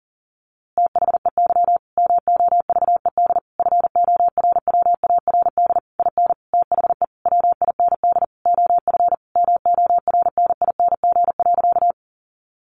Morse code, heard as a keyed tone, about 40 wpm, 700 Hz. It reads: THEY MOVED FORWARD IN THE WIND OF MORNING.
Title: A Sound Of Thunder 40wpm